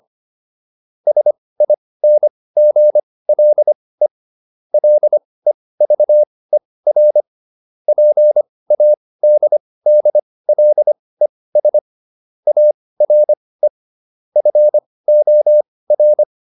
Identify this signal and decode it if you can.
Morse code, heard as a keyed tone, about 25 wpm, spaced as for 18 wpm, 600 Hz. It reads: SINGLE LEVER PADDLES ARE FOR